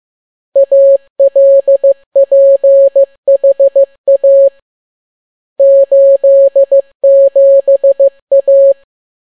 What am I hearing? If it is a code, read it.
Morse code, heard as a keyed tone, about 15 wpm, 550 Hz. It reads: ALPHA 87A